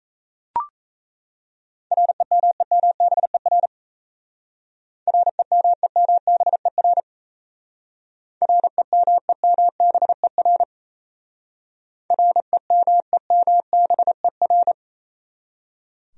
Morse code, heard as a keyed tone, about 36 wpm, 700 Hz. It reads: REMEMBER REMEMBER REMEMBER REMEMBER